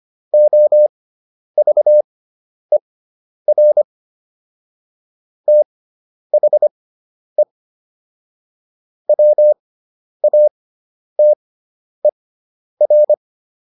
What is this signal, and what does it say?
Morse code, heard as a keyed tone, about 25 wpm, spaced as for 10 wpm, 600 Hz. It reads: OVER THE WATER